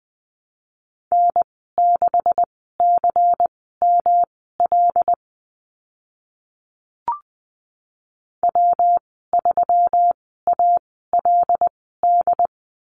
Morse code, heard as a keyed tone, about 20 wpm, spaced as for 14 wpm, 700 Hz. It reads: N6CML W3ALD
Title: 147 Lesson 24g - Letter y - Call Signs 20wpm